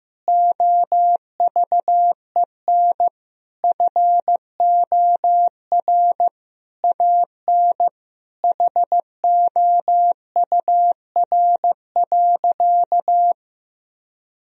Morse code, heard as a keyed tone, about 15 wpm, 700 Hz. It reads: OVEN FOR AN HOUR.
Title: Sentences from Top 1000 Words - Review Part 4 - 15wpm